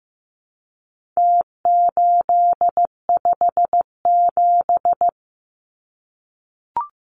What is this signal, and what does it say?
Morse code, heard as a keyed tone, about 15 wpm, 700 Hz. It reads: T857
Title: Numbers - Contest Style - 4-Digit 15wpm